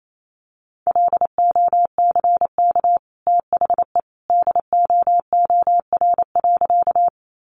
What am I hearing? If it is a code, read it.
Morse code, heard as a keyed tone, about 28 wpm, 700 Hz. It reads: LOCK THE DOOR.
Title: Sentences from Top 1300 Words 28wpm